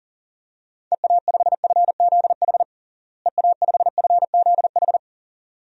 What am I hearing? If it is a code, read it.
Morse code, heard as a keyed tone, about 40 wpm, 700 Hz. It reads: EA5FZH EA5FZH
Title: DXCC-100 Call Signs - 6 Characters - Repeated 2x - 40wpm